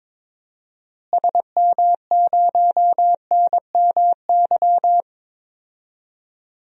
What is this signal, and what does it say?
Morse code, heard as a keyed tone, about 22 wpm, 700 Hz. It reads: SM0NMY
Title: DXCC-100 Call Signs - 6 Characters - Repeated 2x - 22wpm